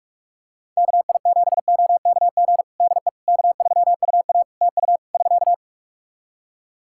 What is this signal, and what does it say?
Morse code, heard as a keyed tone, about 47 wpm, 700 Hz. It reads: KI7OKG DE K3UA TU <SK>
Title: Practice QSO - Exchange 4 - Ending - 45wpm